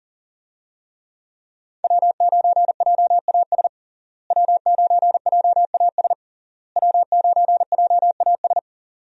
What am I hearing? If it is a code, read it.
Morse code, heard as a keyed tone, about 40 wpm, 700 Hz. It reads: W9JAS W9JAS W9JAS
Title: USA Call Signs - 5 Characters - Repeated 3x - 40wpm